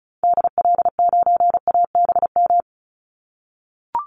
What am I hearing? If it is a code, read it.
Morse code, heard as a keyed tone, about 34 wpm, 700 Hz. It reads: DL9ABM
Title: Call Signs - 35wpm